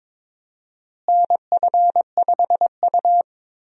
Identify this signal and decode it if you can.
Morse code, heard as a keyed tone, about 22 wpm, 700 Hz. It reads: NF5U
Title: USA Call Signs - 4 Characters - Repeated 2x - 22wpm